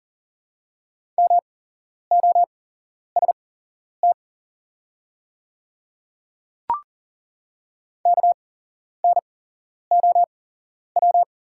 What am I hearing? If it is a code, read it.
Morse code, heard as a keyed tone, about 37 wpm, spaced as for 11 wpm, 690 Hz. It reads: MOST KNOW
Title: 8x Character Spacing - Top 100 Words - 40wpm